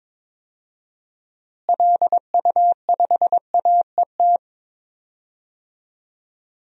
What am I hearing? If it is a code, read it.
Morse code, heard as a keyed tone, about 22 wpm, 700 Hz. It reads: LU5AET
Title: DXCC-100 Call Signs - 6 Characters - Repeated 2x - 22wpm